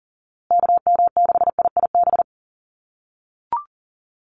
Morse code, heard as a keyed tone, about 40 wpm, 700 Hz. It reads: KM6IIB
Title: USA Call Signs Repeated 2x 40wpm